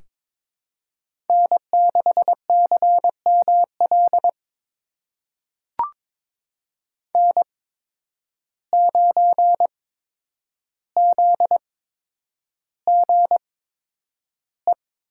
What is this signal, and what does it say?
Morse code, heard as a keyed tone, about 22 wpm, spaced as for 9 wpm, 700 Hz. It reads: N6CML N9ZGE
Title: USA Call Signs - 5 Characters - 8x Character Spacing - 22wpm